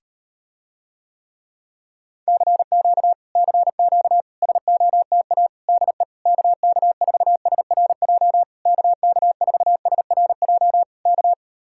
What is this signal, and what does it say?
Morse code, heard as a keyed tone, about 39 wpm, 700 Hz. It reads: CQ CQ SOTA DE KK4SRJ KK4SRJ K